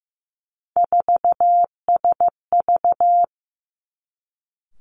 Morse code, heard as a keyed tone, about 15 wpm, 700 Hz. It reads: 4SV
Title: Three Challenging Letter-Number Combinations 15wpm